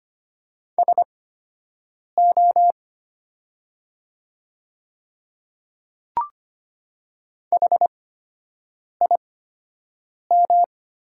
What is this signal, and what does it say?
Morse code, heard as a keyed tone, about 25 wpm, spaced as for 7 wpm, 700 Hz. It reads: SO HIM
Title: 8x Character Spacing - Top 100 Words - 25wpm